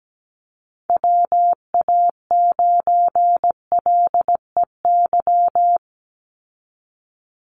Morse code, heard as a keyed tone, about 17 wpm, 700 Hz. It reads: WA9LEY
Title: Call Signs Repeated 2x - 17wpm